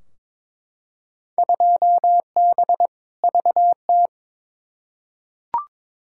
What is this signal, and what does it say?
Morse code, heard as a keyed tone, about 23 wpm, 700 Hz. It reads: 2B VT
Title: Field Day Exchanges Encore - 22wpm